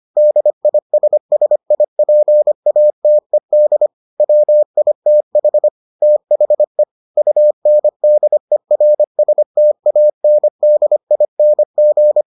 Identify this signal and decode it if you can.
Morse code, heard as a keyed tone, about 25 wpm, 600 Hz. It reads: DISSIPATED WITH THE UNDERSTANDING